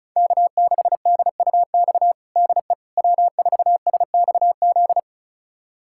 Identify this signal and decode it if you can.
Morse code, heard as a keyed tone, about 36 wpm, 700 Hz. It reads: K6DUX DE W4SXZ
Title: USA Callsign DE Callsign - 5 Characters - 35wpm